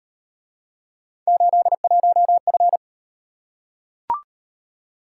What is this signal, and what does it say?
Morse code, heard as a keyed tone, about 37 wpm, 700 Hz. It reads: 81F